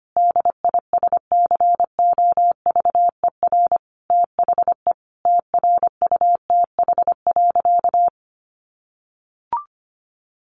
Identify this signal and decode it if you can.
Morse code, heard as a keyed tone, about 25 wpm, 700 Hz. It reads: DISCOVER THE TRUTH.
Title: Sentences from Top 1100 Words 25wpm